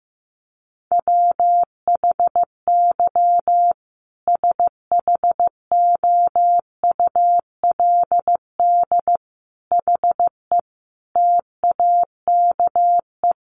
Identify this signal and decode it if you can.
Morse code, heard as a keyed tone, about 15 wpm, 700 Hz. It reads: WHY SHOULD HE TAKE
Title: Sentences from Top 1500 Words 15wpm